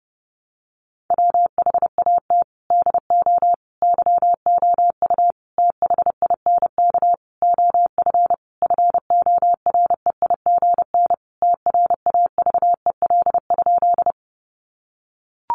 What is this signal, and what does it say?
Morse code, heard as a keyed tone, about 30 wpm, 700 Hz. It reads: WHAT DO YOU THINK OF FOREIGN TRAVEL?
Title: Sentences from Top 1400 Words 30wpm